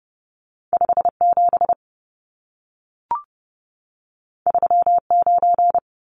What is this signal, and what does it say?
Morse code, heard as a keyed tone, about 30 wpm, 700 Hz. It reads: 57 39
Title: Numbers - 2-Digit 30wpm